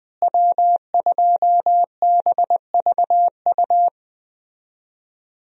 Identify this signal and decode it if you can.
Morse code, heard as a keyed tone, about 20 wpm, 700 Hz. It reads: W2BVU
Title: USA Call Signs - 5 Characters - Repeated 2x - 20wpm